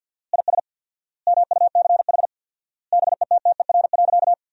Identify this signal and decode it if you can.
Morse code, heard as a keyed tone, about 50 wpm, 700 Hz. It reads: IS MUCH BETTER.
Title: 4x Word-Spacing - Sentences from Top 400 Words - 50wpm